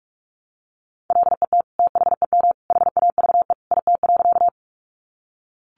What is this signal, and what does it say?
Morse code, heard as a keyed tone, about 44 wpm, 700 Hz. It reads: LET THEM HAVE IT.
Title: Sentences from Top 300 Words 45wpm